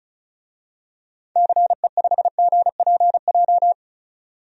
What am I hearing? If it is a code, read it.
Morse code, heard as a keyed tone, about 36 wpm, 700 Hz. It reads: CE5GPJ
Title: DXCC-100 Call Signs - 6 Characters - 35wpm